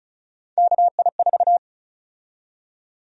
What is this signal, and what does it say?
Morse code, heard as a keyed tone, about 36 wpm, 700 Hz. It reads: KI4
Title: USA Call Sign Prefixes - 35wpm